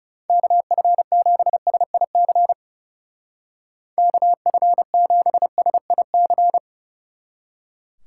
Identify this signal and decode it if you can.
Morse code, heard as a keyed tone, about 36 wpm, 700 Hz. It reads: KF7SIC KF7SIC
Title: Speed Racing - USA Call Signs 30wpm